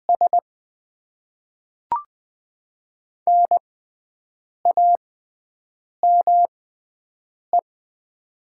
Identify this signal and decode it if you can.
Morse code, heard as a keyed tone, about 20 wpm, spaced as for 7 wpm, 700 Hz. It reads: S NAME
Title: Top 90 QSO Elements - 6x Character Spacing - 20wpm